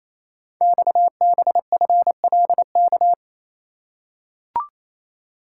Morse code, heard as a keyed tone, about 29 wpm, 700 Hz. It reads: XBFLK
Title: Code Groups - 5 Letters - 28wpm